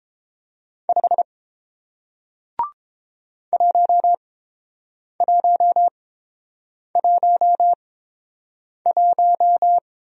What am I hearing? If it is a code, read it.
Morse code, heard as a keyed tone, about 30 wpm, spaced as for 7 wpm, 700 Hz. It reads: 5 1111